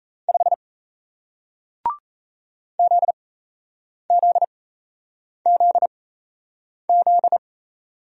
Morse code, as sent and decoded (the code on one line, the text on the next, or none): ..... / --.. --.. --.. --..
5 ZZZZ